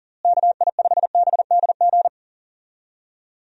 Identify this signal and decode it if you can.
Morse code, heard as a keyed tone, about 37 wpm, 700 Hz. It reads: KI5BDG
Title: USA Call Signs - 6 Characters - Repeated 2x - 40wpm